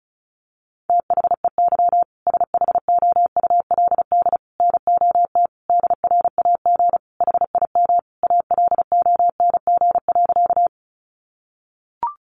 Morse code, heard as a keyed tone, about 34 wpm, 700 Hz. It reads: THEY SHOULD NOT DRAG HIM ALONG.
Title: Sentences from Top 1500 Words 35wpm